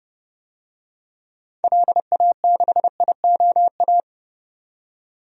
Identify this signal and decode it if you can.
Morse code, heard as a keyed tone, about 29 wpm, 690 Hz. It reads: LA6IOA
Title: DXCC-100 Call Signs - 6 Characters - 30wpm